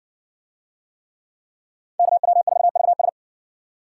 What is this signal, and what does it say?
Morse code, heard as a keyed tone, about 56 wpm, 700 Hz. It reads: KW4VS